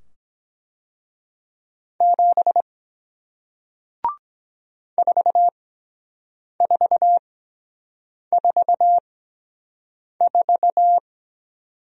Morse code, heard as a keyed tone, about 23 wpm, spaced as for 7 wpm, 700 Hz. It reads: 7 4444